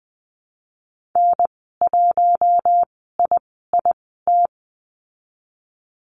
Morse code, heard as a keyed tone, about 20 wpm, spaced as for 15 wpm, 700 Hz. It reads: N1IIT
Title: 048 Lesson 9g - Number 4 - Call Signs 20wpm